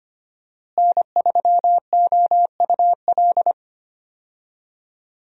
Morse code, heard as a keyed tone, about 25 wpm, 700 Hz. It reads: N3OUL
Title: USA Call Signs - 5 Characters - Repeated 3x - 25wpm